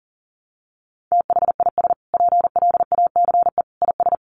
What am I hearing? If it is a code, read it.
Morse code, heard as a keyed tone, about 40 wpm, 700 Hz. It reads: THIS PLACE IS
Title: Sentences from Top 1200 Words 40wpm